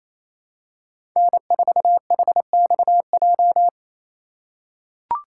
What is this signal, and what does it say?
Morse code, heard as a keyed tone, about 28 wpm, 700 Hz. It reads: N4HXJ
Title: USA Call Signs - 5 Characters - 6x Character Spacing - 28wpm